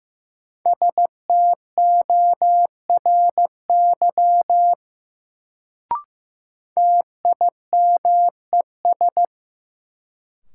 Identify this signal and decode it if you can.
Morse code, heard as a keyed tone, about 15 wpm, 700 Hz. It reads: STORY TIMES
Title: Sets of 3 Words - New Word Drill - 15wpm